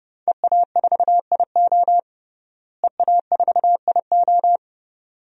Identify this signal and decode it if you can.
Morse code, heard as a keyed tone, about 32 wpm, 700 Hz. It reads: EA4IO EA4IO
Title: DXCC-100 Call Signs - 5 Characters - Repeated 2x - 30wpm